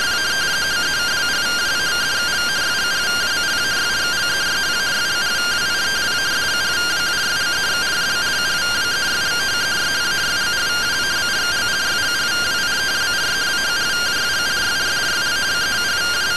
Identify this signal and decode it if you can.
Two-tone teleprinter signal, about 170 Hz shift.